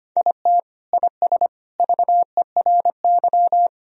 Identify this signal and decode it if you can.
Morse code, heard as a keyed tone, about 25 wpm, 700 Hz. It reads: IT IS VERY
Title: Sentences from Top 1000 Words - Review Part 1 - 25wpm